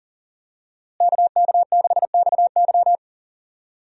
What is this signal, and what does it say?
Morse code, heard as a keyed tone, about 40 wpm, 700 Hz. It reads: KK6XY
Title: USA Call Signs - 5 Characters - Repeated 2x - 40wpm